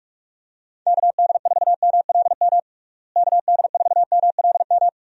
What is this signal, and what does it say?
Morse code, heard as a keyed tone, about 41 wpm, 700 Hz. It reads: KD4MLM KD4MLM
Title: USA Call Signs - 6 Characters - Repeated 2x - 45wpm